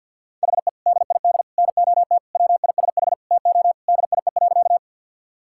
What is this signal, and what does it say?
Morse code, heard as a keyed tone, about 52 wpm, 700 Hz. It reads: HE DID NOT WISH TO DIE.
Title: Sentences from Top 1600 Words 50wpm